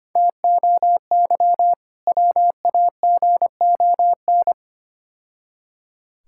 Morse code, heard as a keyed tone, about 27 wpm, 700 Hz. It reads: TOY WAGON
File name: practice / 25wpm / Difficult Words - Speed-Racing with Hints - 25wpm